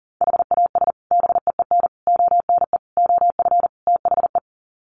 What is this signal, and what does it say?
Morse code, heard as a keyed tone, about 40 wpm, 700 Hz. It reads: HAS BEEN ONE OF THE
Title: Sets of 5 Words Encore 40wpm